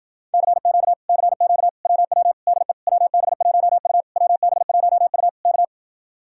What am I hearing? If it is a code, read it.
Morse code, heard as a keyed tone, about 52 wpm, 690 Hz. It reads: CQ CQ WW DE WB1U WB1U K